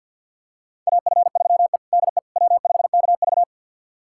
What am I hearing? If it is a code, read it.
Morse code, heard as a keyed tone, about 52 wpm, 700 Hz. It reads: AF3E DE W5KV
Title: USA Callsign DE Callsign - 4 Characters - 50wpm